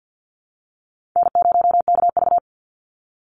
Morse code, heard as a keyed tone, about 50 wpm, 700 Hz. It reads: N0KV
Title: Call Signs Repeated 3x - 50wpm